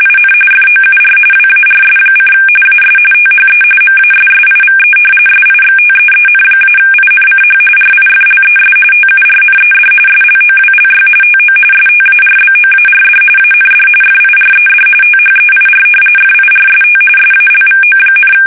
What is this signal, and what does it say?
RTTY, about 45.45 baud. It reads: IN FOUR WARS: AS THE MASSIVE SIXTEEN-INCH RIFLES TURNED SLOWLY TO STARBOARD AND FIRED THEIR FIRST SALVO,
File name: fsk850 / battleshipnj